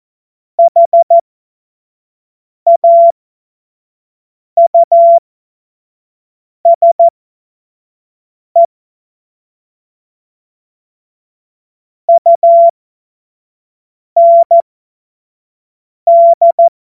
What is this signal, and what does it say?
Morse code, heard as a keyed tone, about 14 wpm, spaced as for 5 wpm, 680 Hz. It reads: HAUSE UND
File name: Morsekurs / Witze / Witz 15. Februar 2022 25 F70